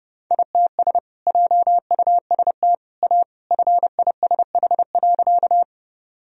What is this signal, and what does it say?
Morse code, heard as a keyed tone, about 30 wpm, 700 Hz. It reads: ITS JUST A FISH.
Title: Sentences from Top 1000 Words - Review Part 1 - 30wpm